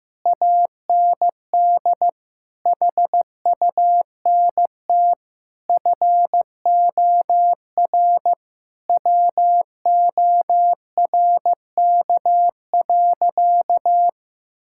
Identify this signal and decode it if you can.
Morse code, heard as a keyed tone, about 15 wpm, 700 Hz. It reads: AND HUNT FOR WORK.
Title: Sentences from Top 1600 Words 15wpm